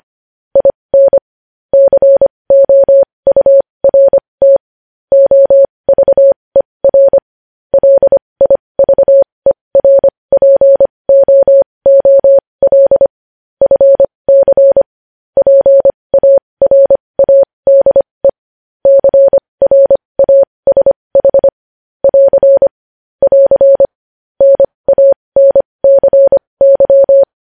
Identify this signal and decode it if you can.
Morse code, heard as a keyed tone, about 25 wpm, spaced as for 20 wpm, 550 Hz. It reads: IN COURT OVER LIVERPOOL FC PARADE CRASH <AR> <AR> NANCY
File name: podcast / News headlines-1x-25x20